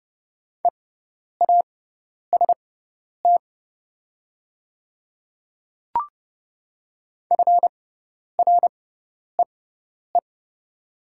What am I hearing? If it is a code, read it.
Morse code, heard as a keyed tone, about 30 wpm, spaced as for 10 wpm, 690 Hz. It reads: EAST FREE
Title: 6x Character Spacing - Top 500 Words - 30wpm